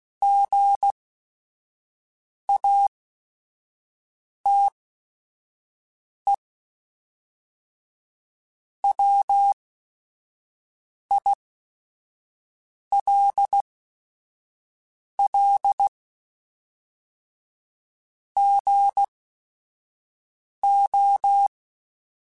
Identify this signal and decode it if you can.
Morse code, heard as a keyed tone, about 16 wpm, spaced as for 5 wpm, 800 Hz. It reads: GATE WILL GO
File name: Technical Subjects / Technical Subjects-mp3s / AND Gate @5